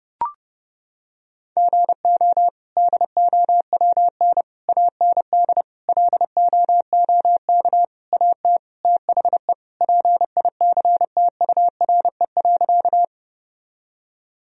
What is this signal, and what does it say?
Morse code, heard as a keyed tone, about 30 wpm, 700 Hz. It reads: GO DOWN AND LOOK AT THE PICTURE.
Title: Sentences from Top 1000 Words - Review Part 1 - 30wpm